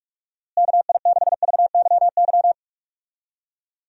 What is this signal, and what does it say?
Morse code, heard as a keyed tone, about 46 wpm, 700 Hz. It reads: KI6VYY
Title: USA Call Signs - 6 Characters - Repeated 3x - 45wpm